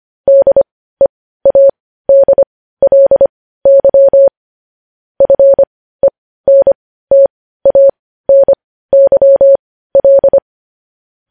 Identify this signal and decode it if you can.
Morse code, heard as a keyed tone, about 25 wpm, spaced as for 15 wpm, 550 Hz. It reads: DEADLY FENTANYL
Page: news headlines in morse code at 15 wpm